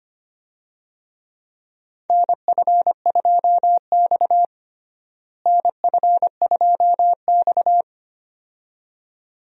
Morse code, heard as a keyed tone, about 25 wpm, 700 Hz. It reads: NF2X NF2X
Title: USA Call Signs - 4 Characters - Repeated 2x - 25wpm